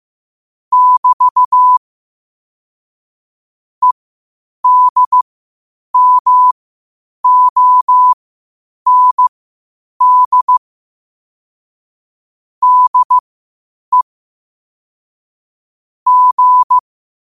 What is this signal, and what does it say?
Morse code, heard as a keyed tone, about 15 wpm, spaced as for 8 wpm, 1000 Hz. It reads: <BT> EDMOND DE G
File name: qotdfiles / 8wpm